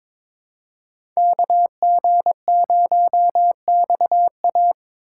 Morse code, heard as a keyed tone, about 22 wpm, 700 Hz. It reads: KG0XA
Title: USA Call Signs - 5 Characters - 3x Character Spacing - 22wpm